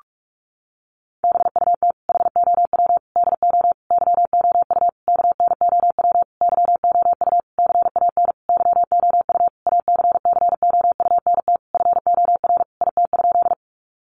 Morse code, heard as a keyed tone, about 44 wpm, 700 Hz. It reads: BUT HOW DO YOU KNOW YOU CAN YOU ACCOUNT FOR IT?
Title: Sentences from Top 900 Words 45wpm